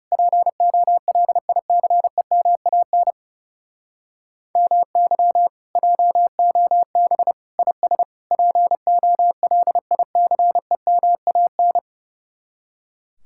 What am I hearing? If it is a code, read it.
Morse code, heard as a keyed tone, about 39 wpm, 700 Hz. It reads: POLICEMAN MY JOB IS POLICEMAN